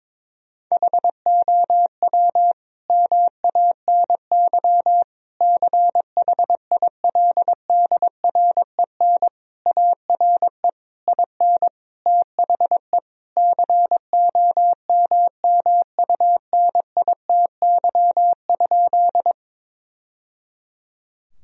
Morse code, heard as a keyed tone, about 22 wpm, 700 Hz. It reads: HOW MANY CHILDREN ARE IN THE COMMUNITY?
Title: Sentences from Top 1000 Words - Review Part 4 - 22wpm